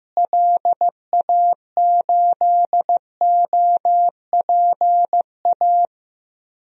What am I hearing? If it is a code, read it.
Morse code, heard as a keyed tone, about 15 wpm, 700 Hz. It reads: LA8OPA
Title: DXCC-100 Call Signs - 6 Characters - Repeated 2x - 15wpm